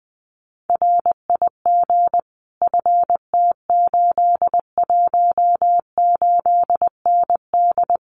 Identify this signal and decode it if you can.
Morse code, heard as a keyed tone, about 20 wpm, 700 Hz. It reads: RIG FT818ND
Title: Common QSO Phrases Encore 20wpm